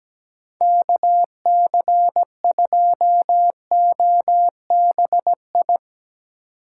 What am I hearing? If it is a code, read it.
Morse code, heard as a keyed tone, about 17 wpm, 700 Hz. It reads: KC2OBI
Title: Speed Racing - USA 2x3 Call Signs 17wpm